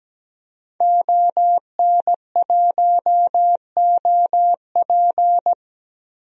Morse code, heard as a keyed tone, about 17 wpm, 700 Hz. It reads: ON1OP
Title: DXCC-100 Call Signs - 5 Characters - Repeated 2x - 17wpm